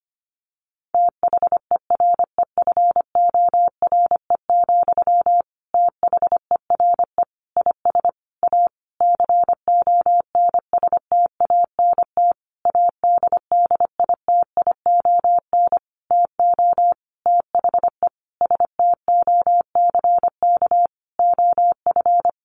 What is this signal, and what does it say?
Morse code, heard as a keyed tone, about 25 wpm, 700 Hz. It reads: THEREFORE, THERE IS A CONSTANT ADDITION TO THE STOCK OF
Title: Sentences from Top 1500 Words 25wpm